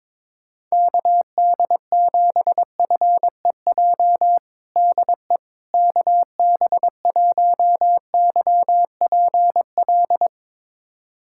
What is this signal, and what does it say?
Morse code, heard as a keyed tone, about 22 wpm, 700 Hz. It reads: KD7FEJ DE KB1YPL
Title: USA Callsign DE Callsign - 6 Characters - 22wpm